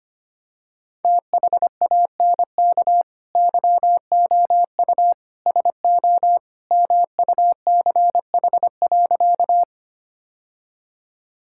Morse code, heard as a keyed tone, about 25 wpm, 700 Hz. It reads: THANK YOU SO MUCH.
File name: practice / 25wpm / Sentences with 2-Word Phrases - 25wpm